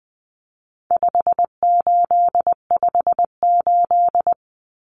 Morse code, heard as a keyed tone, about 19 wpm, 700 Hz. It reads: 5858
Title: Numbers - 4-Digit 20wpm